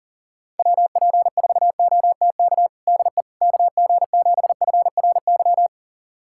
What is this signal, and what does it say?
Morse code, heard as a keyed tone, about 37 wpm, 700 Hz. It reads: WP4OTK DE KG7FRY
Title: USA Callsign DE Callsign - 6 Characters - 40wpm